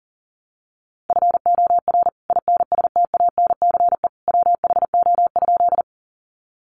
Morse code, heard as a keyed tone, about 40 wpm, 700 Hz. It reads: FOR INSTANCE WHO?
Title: Sentences from Top 800 Words 40wpm